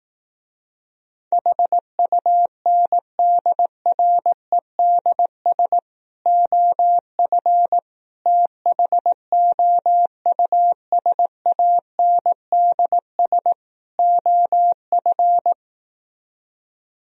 Morse code, heard as a keyed tone, about 18 wpm, 700 Hz. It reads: HUNDREDS OF THOUSANDS OF
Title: Speed-Racing - Sets of 4 Words 15wpm